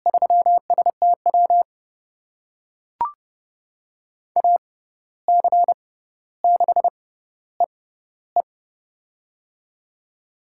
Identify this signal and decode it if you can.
Morse code, heard as a keyed tone, about 30 wpm, spaced as for 14 wpm, 700 Hz. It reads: 3STW AC6EE
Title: USA Call Signs - 5 Characters - 6x Character Spacing - 30wpm